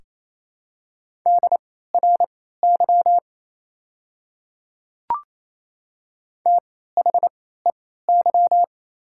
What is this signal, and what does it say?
Morse code, heard as a keyed tone, about 28 wpm, spaced as for 16 wpm, 700 Hz. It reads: DRY THEY
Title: 3x Character Spacing - Top 500 Words - 28wpm